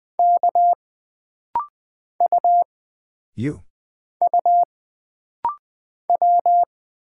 Morse code, heard as a keyed tone, about 20 wpm, 700 Hz. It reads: K U U W
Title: Single Character (Letter:Number:Punctuation:Prosign) - Rapid-Fire - 20wpm